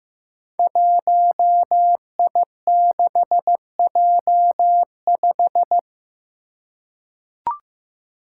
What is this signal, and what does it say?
Morse code, heard as a keyed tone, about 15 wpm, 700 Hz. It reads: 1I6J5
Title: Five Challenging Letter-Number Combinations 15wpm